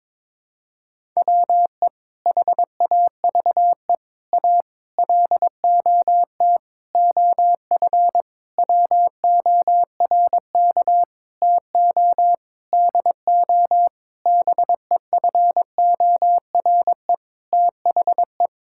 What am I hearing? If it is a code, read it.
Morse code, heard as a keyed tone, about 22 wpm, 700 Hz. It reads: WE HAVE A LOT OF WORK TO DO BEFORE THE
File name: practice / 22wpm / Sentences with 8-Word Phrases - 22wpm